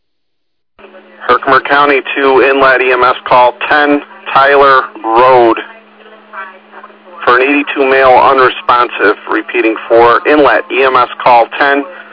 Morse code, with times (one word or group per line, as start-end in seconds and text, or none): none